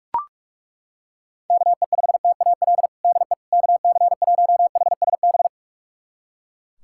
Morse code, heard as a keyed tone, about 42 wpm, 700 Hz. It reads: KE5TAL DE KC1HSB
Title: USA Callsign DE Callsign - 6 Characters - 45wpm